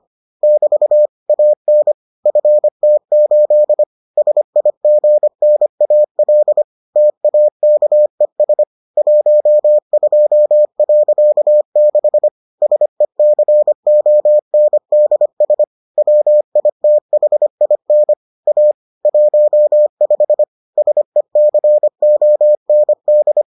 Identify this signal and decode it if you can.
Morse code, heard as a keyed tone, about 25 wpm, 600 Hz. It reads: <BT> AN FT8 SIGNAL TAKES 12.6 SECONDS WITHIN A 15 SECOND